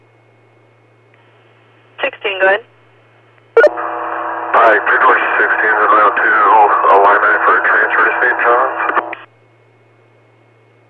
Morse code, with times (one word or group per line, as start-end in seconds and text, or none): none